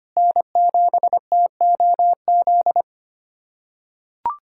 Morse code, heard as a keyed tone, about 25 wpm, 700 Hz. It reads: N7TOZ
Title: USA Callsign DE Callsign - 5 Characters - 25wpm